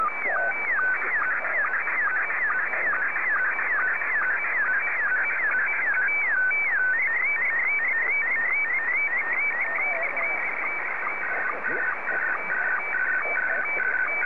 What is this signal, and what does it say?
SSTV (slow-scan television), Scottie 1, with speech.